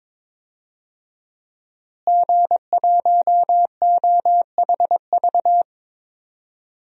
Morse code, heard as a keyed tone, about 22 wpm, 700 Hz. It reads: G1OHV
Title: DXCC-100 Call Signs - 5 Characters - Repeated 2x - 22wpm